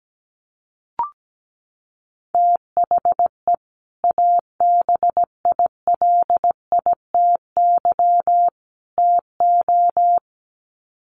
Morse code, heard as a keyed tone, about 17 wpm, 700 Hz. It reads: THE ABILITY TO
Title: Sets of 3 Words 17wpm